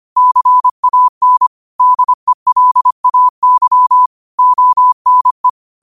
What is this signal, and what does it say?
Morse code, heard as a keyed tone, about 25 wpm, 1000 Hz. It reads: CAN DELAY ONE
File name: qotdfiles / 25wpm